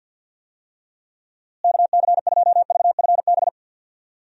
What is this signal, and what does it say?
Morse code, heard as a keyed tone, about 46 wpm, 700 Hz. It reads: KC2VFB
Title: USA Call Signs - 6 Characters - Repeated 2x - 50wpm